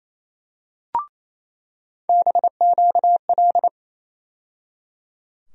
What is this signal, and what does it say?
Morse code, heard as a keyed tone, about 28 wpm, 700 Hz. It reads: BQL
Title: USA Call Sign Postfixes - 3 Letters 28wpm